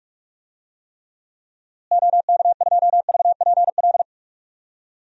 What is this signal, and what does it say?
Morse code, heard as a keyed tone, about 42 wpm, 700 Hz. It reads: OK2VPL
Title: DXCC-100 Call Signs - 6 Characters - Repeated 2x - 45wpm